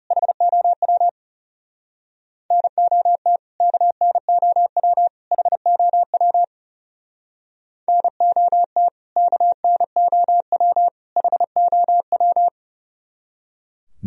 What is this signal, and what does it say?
Morse code, heard as a keyed tone, about 37 wpm, 700 Hz. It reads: HOW NOT KNOW HOW NOT KNOW HOW
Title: Speed-Racing - Sets of 3 Words 30wpm